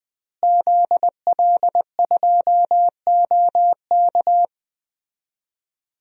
Morse code, heard as a keyed tone, about 19 wpm, 700 Hz. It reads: ZL2OK
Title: DXCC-100 Call Signs - 5 Characters - Repeated 3x - 20wpm